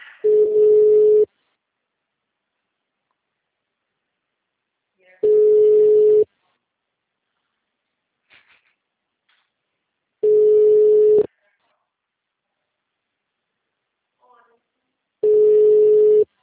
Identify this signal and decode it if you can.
Ringback tone (Europe, 425 Hz), with speech.
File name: out- -2004- - - .8